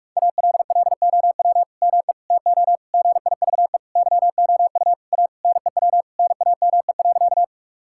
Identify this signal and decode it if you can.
Morse code, heard as a keyed tone, about 46 wpm, 700 Hz. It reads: ALLOW ME TO GIVE YOU A NEW NAME.
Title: Sentences from Top 1000 Words - Review Part 5 - 45wpm